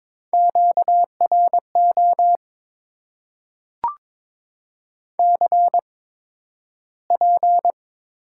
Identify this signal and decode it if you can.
Morse code, heard as a keyed tone, about 22 wpm, spaced as for 12 wpm, 700 Hz. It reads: QRO CP
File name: Top 90 QSO Elements - 8x Character Spacing - 22wpm